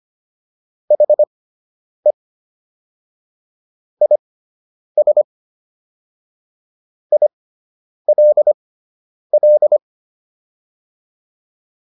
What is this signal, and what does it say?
Morse code, heard as a keyed tone, about 25 wpm, spaced as for 9 wpm, 600 Hz. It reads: HE IS ILL